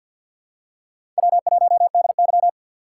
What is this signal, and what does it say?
Morse code, heard as a keyed tone, about 53 wpm, 700 Hz. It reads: W1DY
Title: USA Call Signs - 4 Characters - 6x Character Spacing - 50wpm